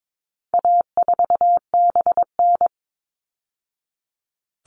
Morse code, heard as a keyed tone, about 22 wpm, 700 Hz. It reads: A4BN
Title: All Cut Numbers - 4-Digit 22wpm